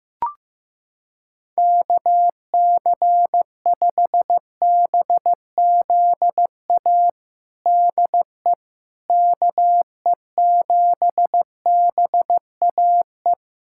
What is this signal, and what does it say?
Morse code, heard as a keyed tone, about 15 wpm, 700 Hz. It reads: KC5BZA DE KE7BAE
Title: USA Callsign DE Callsign 15wpm